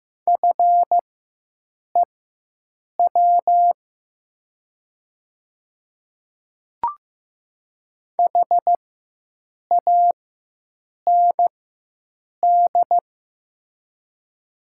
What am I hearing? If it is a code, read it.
Morse code, heard as a keyed tone, about 15 wpm, spaced as for 7 wpm, 700 Hz. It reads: FEW HAND
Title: 4x Character Spacing - Top 300 Words - 15wpm